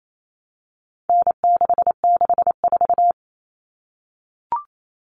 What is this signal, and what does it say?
Morse code, heard as a keyed tone, about 28 wpm, 700 Hz. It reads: N664
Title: All Cut Numbers - 4-Digit 28wpm